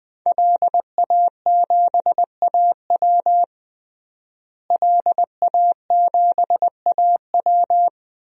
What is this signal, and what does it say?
Morse code, heard as a keyed tone, about 20 wpm, 700 Hz. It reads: LA7AW LA7AW
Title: DXCC-100 Call Signs - 5 Characters - Repeated 3x - 20wpm